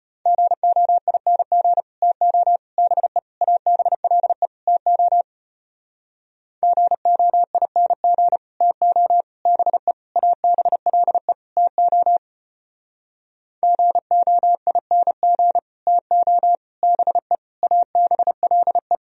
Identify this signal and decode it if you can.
Morse code, heard as a keyed tone, about 33 wpm, 700 Hz. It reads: GOING TO BE ABLE TO GOING TO BE ABLE TO GOING TO BE ABLE